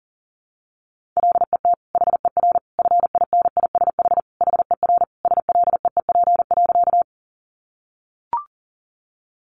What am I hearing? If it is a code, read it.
Morse code, heard as a keyed tone, about 40 wpm, 700 Hz. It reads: LET HER FINISH HER SLEEP.
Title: Sentences from Top 1200 Words 40wpm